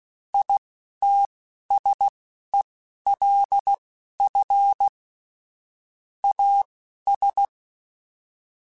Morse code, heard as a keyed tone, about 16 wpm, spaced as for 11 wpm, 800 Hz. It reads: ITSELF AS
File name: Technical Subjects / Technical Subjects-mp3s / XC @10